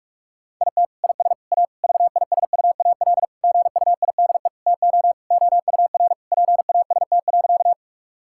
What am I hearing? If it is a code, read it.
Morse code, heard as a keyed tone, about 46 wpm, 700 Hz. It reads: IT IS A VISUAL GUIDE TO OUR PAST.
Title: Sentences from Top 1000 Words - Review Part 6 - 45wpm